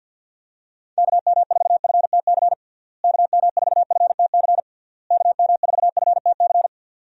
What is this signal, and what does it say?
Morse code, heard as a keyed tone, about 50 wpm, 700 Hz. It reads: KM4FTC KM4FTC KM4FTC
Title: USA Call Signs - 6 Characters - Repeated 3x - 50wpm